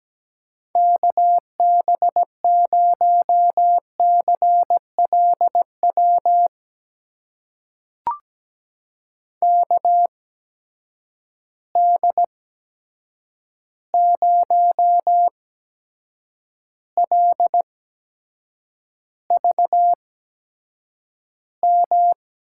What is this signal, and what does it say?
Morse code, heard as a keyed tone, about 17 wpm, spaced as for 7 wpm, 700 Hz. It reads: KB0CLW KD0LVM
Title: USA Call Signs - 6 Characters - 8x Character Spacing - 17wpm